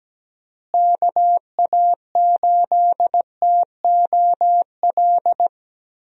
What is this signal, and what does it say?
Morse code, heard as a keyed tone, about 17 wpm, 700 Hz. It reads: KA8TOL
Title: USA Call Signs - 6 Characters - Repeated 3x - 17wpm